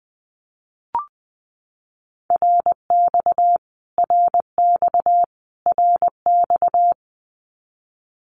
Morse code, heard as a keyed tone, about 20 wpm, 700 Hz. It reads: RX RX RX
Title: Abbreviations Repeated 3x - 20wpm